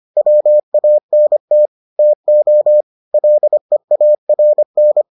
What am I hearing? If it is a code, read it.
Morse code, heard as a keyed tone, about 25 wpm, 600 Hz. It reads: WANT TO LEARN